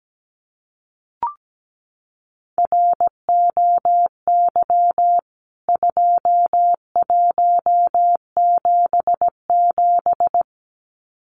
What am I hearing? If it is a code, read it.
Morse code, heard as a keyed tone, about 17 wpm, 700 Hz. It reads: ROY 2177
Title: CWT Exchange 17wpm